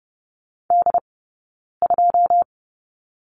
Morse code, heard as a keyed tone, about 30 wpm, spaced as for 9 wpm, 690 Hz. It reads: D2
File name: Lesson 17d - Letter u - Two Characters 30wpm